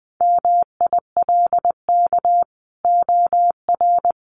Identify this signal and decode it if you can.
Morse code, heard as a keyed tone, about 20 wpm, 700 Hz. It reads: MILK OR
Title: Sentences from Top 1500 Words 20wpm